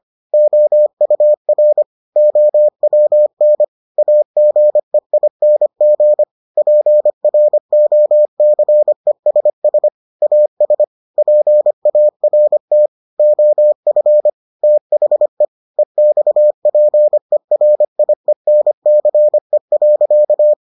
Morse code, heard as a keyed tone, about 25 wpm, 600 Hz. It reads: OUR OWN AGEING PROCESS AS PART OF THE EXPERIENCE.